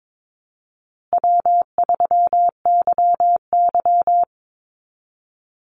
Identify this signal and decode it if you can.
Morse code, heard as a keyed tone, about 22 wpm, 700 Hz. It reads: W3YY
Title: Call Signs Repeated 2x - 22wpm